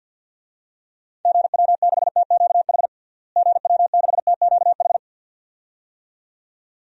Morse code, heard as a keyed tone, about 50 wpm, 700 Hz. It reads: GW6TQH GW6TQH
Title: DXCC-100 Call Signs - 6 Characters - Repeated 2x - 50wpm